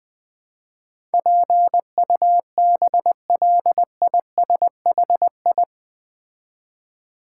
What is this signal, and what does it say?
Morse code, heard as a keyed tone, about 20 wpm, 700 Hz. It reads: PUBLISHI
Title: Build A Word - 4 Letters At A Time - 10-Letter Words - 20wpm